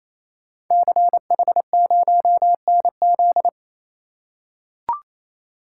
Morse code, heard as a keyed tone, about 27 wpm, 700 Hz. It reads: CH0NZ
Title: Code Groups - 5 Letter:Numbers - 28wpm